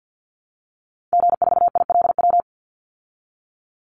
Morse code, heard as a keyed tone, about 50 wpm, 700 Hz. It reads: G4ILW
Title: Call Signs Repeated 2x - 50wpm